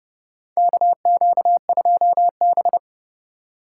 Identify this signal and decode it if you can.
Morse code, heard as a keyed tone, about 30 wpm, 700 Hz. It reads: KQ2B
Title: USA Call Signs - 4 Characters - 3x Character Spacing - 30wpm